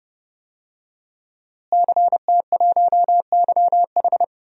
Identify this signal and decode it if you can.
Morse code, heard as a keyed tone, about 30 wpm, 700 Hz. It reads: CT1YH
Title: DXCC-100 Call Signs - 5 Characters - Repeated 2x - 30wpm